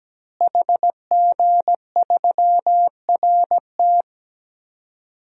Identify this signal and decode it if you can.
Morse code, heard as a keyed tone, about 17 wpm, 700 Hz. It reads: HG3RT
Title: DXCC-100 Call Signs - 5 Characters - Repeated 3x - 17wpm